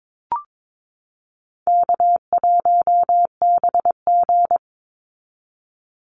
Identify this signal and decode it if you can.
Morse code, heard as a keyed tone, about 22 wpm, 700 Hz. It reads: K1BG
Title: Easy Call Signs Repeated 3x - 22wpm